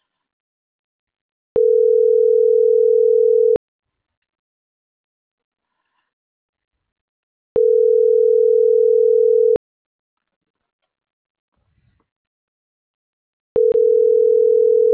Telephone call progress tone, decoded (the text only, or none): ringback tone